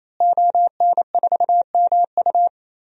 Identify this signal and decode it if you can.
Morse code, heard as a keyed tone, about 27 wpm, 700 Hz. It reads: ON4MU
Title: DXCC-100 Call Signs - 5 Characters - Repeated 3x - 28wpm